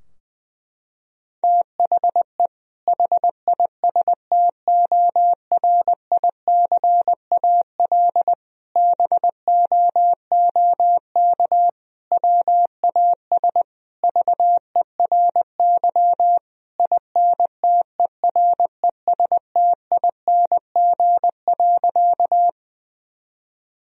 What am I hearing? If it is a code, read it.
Morse code, heard as a keyed tone, about 21 wpm, 700 Hz. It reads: THE HISTORICAL BOOK WAS VERY INTERESTING.
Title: Sentences from Top 1000 Words - Review Part 4 - 20wpm